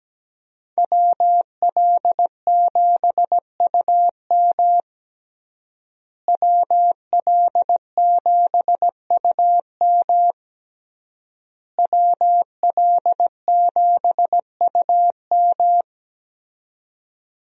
Morse code, heard as a keyed tone, about 17 wpm, 700 Hz. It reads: WL7UM WL7UM WL7UM
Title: USA Call Signs - 5 Characters - Repeated 3x - 17wpm